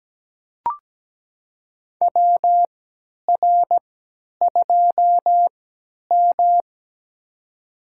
Morse code, heard as a keyed tone, about 17 wpm, spaced as for 10 wpm, 700 Hz. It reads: WR2M
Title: USA Call Signs - 4 Characters - 3x Character Spacing - 17wpm